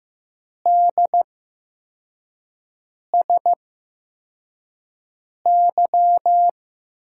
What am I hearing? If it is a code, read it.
Morse code, heard as a keyed tone, about 15 wpm, spaced as for 4 wpm, 700 Hz. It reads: DSY
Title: USA Call Signs - 6 Characters - 8x Character Spacing - 15wpm